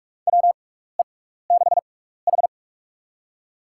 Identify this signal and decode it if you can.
Morse code, heard as a keyed tone, about 46 wpm, spaced as for 16 wpm, 700 Hz. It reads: WE6H
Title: USA Call Signs - 4 Characters - 6x Character Spacing - 45wpm